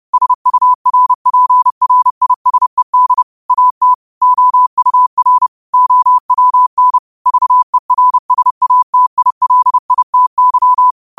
Morse code, heard as a keyed tone, about 30 wpm, 1000 Hz. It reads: SURPRISED AT OUR OWN VERSATILITY
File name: qotdfiles / 30wpm